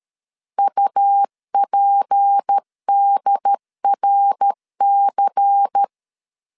Morse code, heard as a keyed tone, about 13 wpm, 800 Hz. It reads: UPDRC
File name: LEZIONI / 6 PKX0 / 6 P 12wpm 2033